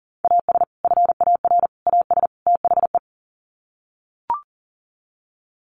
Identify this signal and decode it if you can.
Morse code, heard as a keyed tone, about 40 wpm, 700 Hz. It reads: AS FAR AS THE
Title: Sets of 4 Words 40wpm